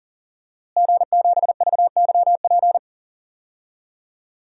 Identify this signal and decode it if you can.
Morse code, heard as a keyed tone, about 40 wpm, 700 Hz. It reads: G7VYP
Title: DXCC-100 Call Signs - 5 Characters - Repeated 2x - 40wpm